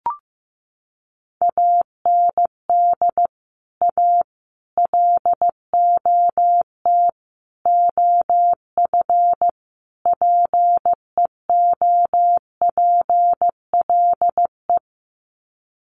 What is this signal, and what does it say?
Morse code, heard as a keyed tone, about 15 wpm, 700 Hz. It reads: AND A LOT OF PEOPLE
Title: Sets of 5 Words 15wpm